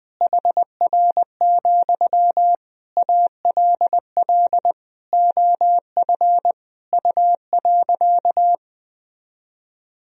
Morse code, heard as a keyed tone, about 20 wpm, 700 Hz. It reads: HR, ALL OF U.
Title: Sentences from Top 500 Words and Abbreviations 20wpm